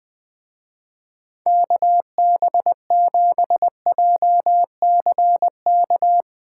Morse code, heard as a keyed tone, about 20 wpm, 700 Hz. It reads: KB7JCK
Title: Speed Racing - USA 2x3 Call Signs 17wpm